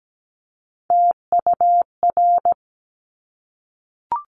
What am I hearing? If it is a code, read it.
Morse code, heard as a keyed tone, about 17 wpm, 700 Hz. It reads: TUR
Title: Top Three Letter Combinations in Words 17wpm